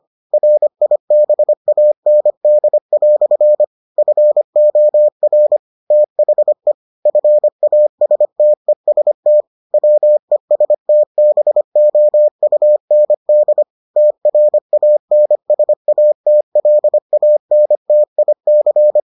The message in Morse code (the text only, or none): RIBAND" FOR THE FASTEST WESTBOUND TRANSATLANTIC